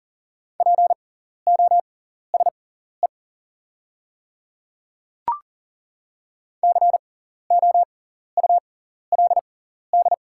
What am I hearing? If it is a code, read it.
Morse code, heard as a keyed tone, about 40 wpm, spaced as for 14 wpm, 690 Hz. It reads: POSE COULD
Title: 6x Character Spacing - Top 500 Words - 40wpm